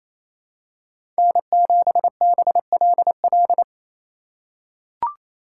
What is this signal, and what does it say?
Morse code, heard as a keyed tone, about 28 wpm, 700 Hz. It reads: N7BLL
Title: USA Call Signs - 5 Characters - 2x Character Spacing - 28wpm